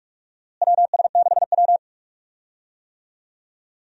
Morse code, heard as a keyed tone, about 44 wpm, 700 Hz. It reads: WS6W
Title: USA Call Signs - 4 Characters - Repeated 3x - 45wpm